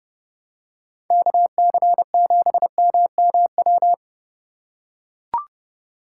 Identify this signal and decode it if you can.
Morse code, heard as a keyed tone, about 30 wpm, 700 Hz. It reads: KC7MMW
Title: USA Call Signs - 4x Character Spacing - 30wpm